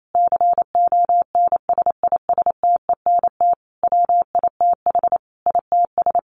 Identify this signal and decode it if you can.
Morse code, heard as a keyed tone, about 28 wpm, 700 Hz. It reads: CONSISTENT WITH ITS
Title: Sentences from Top 1500 Words 28wpm